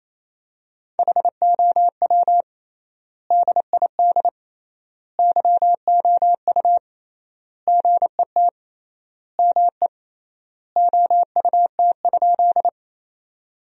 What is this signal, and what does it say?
Morse code, heard as a keyed tone, about 28 wpm, 700 Hz. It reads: HOW DID YOU GET ME OUT?
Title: 3x Word-Spacing - Sentences from Top 200 Words - 28wpm